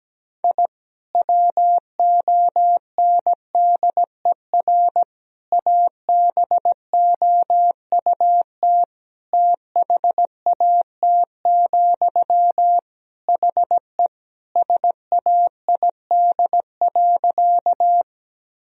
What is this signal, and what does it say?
Morse code, heard as a keyed tone, about 17 wpm, 700 Hz. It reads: I WONDER ABOUT THAT, HE SAID.
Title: Sentences from Top 1000 Words - Review Part 3 - 17wpm